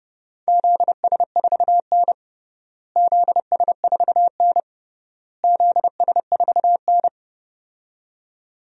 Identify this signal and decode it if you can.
Morse code, heard as a keyed tone, about 32 wpm, 700 Hz. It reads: ZS4N ZS4N ZS4N
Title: DXCC-100 Call Signs - 4 Characters - Repeated 3x - 30wpm